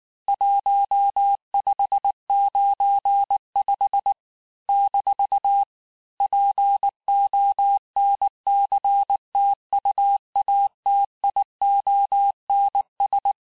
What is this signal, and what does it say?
Morse code, heard as a keyed tone, about 19 wpm, 800 Hz. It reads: 1595 - PONCTUATIONS
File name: cours-du-lundi files / cours-1595-teste-PONCTUATIONS-vitesse18-MPM-puis-23-MPM